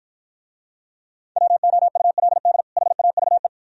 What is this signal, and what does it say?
Morse code, heard as a keyed tone, about 53 wpm, 700 Hz. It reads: WOULD HAVE